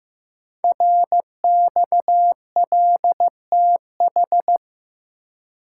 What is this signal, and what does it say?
Morse code, heard as a keyed tone, about 15 wpm, 700 Hz. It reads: RXLTH
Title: Code Groups - 5 Letters - 15wpm